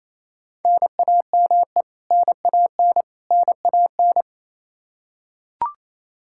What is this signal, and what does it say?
Morse code, heard as a keyed tone, about 27 wpm, 700 Hz. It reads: NAME NAN NAN
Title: K1USN SST QSO Practice 28wpm